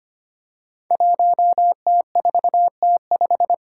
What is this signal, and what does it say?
Morse code, heard as a keyed tone, about 25 wpm, 700 Hz. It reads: 1T4T5
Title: Five Challenging Letter-Number Combinations 25wpm